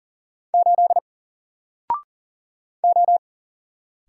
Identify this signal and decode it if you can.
Morse code, heard as a keyed tone, about 40 wpm, 700 Hz. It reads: 8 O
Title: Single Letter-Number 40wpm